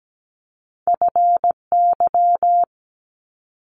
Morse code, heard as a keyed tone, about 17 wpm, 700 Hz. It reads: FY
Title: Multi-letter Phonemes Quick Fire 17wpm